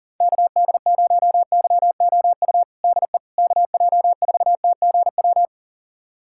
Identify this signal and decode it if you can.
Morse code, heard as a keyed tone, about 37 wpm, 700 Hz. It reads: KD0YOU DE KJ4TGW
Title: USA Callsign DE Callsign - 6 Characters - 40wpm